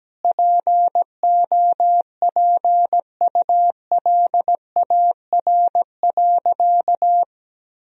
Morse code, heard as a keyed tone, about 17 wpm, 700 Hz. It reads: POPULAR.
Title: Sentences from Top 1000 Words - Review Part 3 - 17wpm